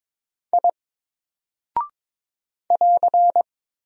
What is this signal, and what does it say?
Morse code, heard as a keyed tone, about 22 wpm, 700 Hz. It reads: I <AR>
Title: Single Character (Letter:Number:Punctuation:Prosign) Alt 22wpm